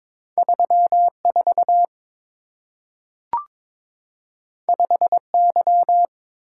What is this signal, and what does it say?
Morse code, heard as a keyed tone, about 22 wpm, 700 Hz. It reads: 34 5Y
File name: practice / 22wpm / Two Letter-Number Combinations - 22wpm